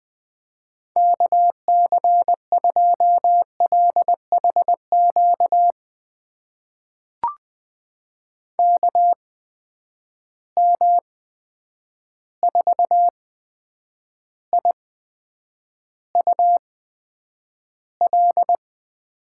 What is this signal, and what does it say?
Morse code, heard as a keyed tone, about 20 wpm, spaced as for 9 wpm, 700 Hz. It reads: KC2LHQ KM4IUL
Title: USA Call Signs - 6 Characters - 8x Character Spacing - 20wpm